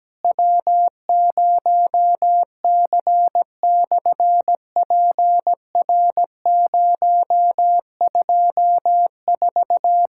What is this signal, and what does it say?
Morse code, heard as a keyed tone, about 17 wpm, 700 Hz. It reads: W0C/PR024
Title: SOTA QSO Practice 17wpm